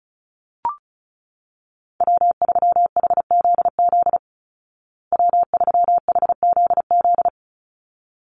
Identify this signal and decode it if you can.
Morse code, heard as a keyed tone, about 34 wpm, 700 Hz. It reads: W3HZZ W3HZZ
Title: Call Signs Repeated 3x - 35wpm